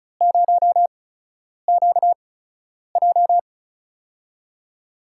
Morse code, heard as a keyed tone, about 34 wpm, spaced as for 10 wpm, 700 Hz. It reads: 0QJ
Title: USA Call Signs - 4 Characters - 8x Character Spacing - 35wpm